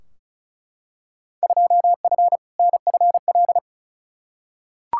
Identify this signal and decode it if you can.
Morse code, heard as a keyed tone, about 34 wpm, 700 Hz. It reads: 2F NFL
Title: Field Day Exchanges - 35wpm